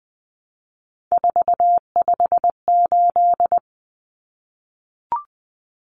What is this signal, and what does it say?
Morse code, heard as a keyed tone, about 20 wpm, 700 Hz. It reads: 458
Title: Numbers - 3-Digit 20wpm